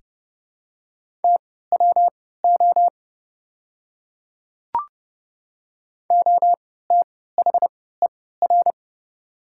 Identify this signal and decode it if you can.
Morse code, heard as a keyed tone, about 31 wpm, spaced as for 17 wpm, 700 Hz. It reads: TWO OTHER
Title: 3x Character Spacing - Top 100 Words - 30wpm